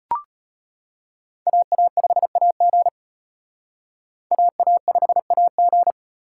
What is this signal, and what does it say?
Morse code, heard as a keyed tone, about 31 wpm, 700 Hz. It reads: AA5AG AA5AG
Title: Speed Racing - USA Call Signs 25wpm